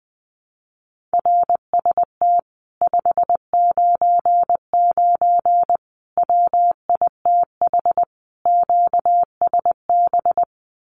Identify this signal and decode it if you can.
Morse code, heard as a keyed tone, about 19 wpm, 700 Hz. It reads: RST 599 WITH QSB
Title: Common QSO Phrases Encore 20wpm